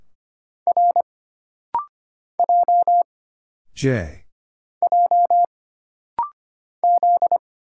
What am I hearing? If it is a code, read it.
Morse code, heard as a keyed tone, about 25 wpm, 700 Hz. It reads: R J J Z
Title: Single Letters - Rapid-Fire 25wpm